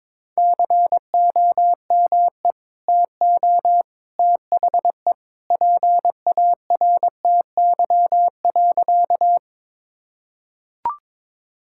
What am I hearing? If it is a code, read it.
Morse code, heard as a keyed tone, about 22 wpm, 700 Hz. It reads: COME TO THE PARTY.
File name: practice / 22wpm / Sentences with 2-Word Phrases - 22wpm